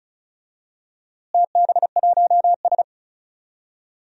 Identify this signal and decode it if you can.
Morse code, heard as a keyed tone, about 36 wpm, 700 Hz. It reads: TB1S
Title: DXCC-100 Call Signs - 4 Characters - Repeated 3x - 35wpm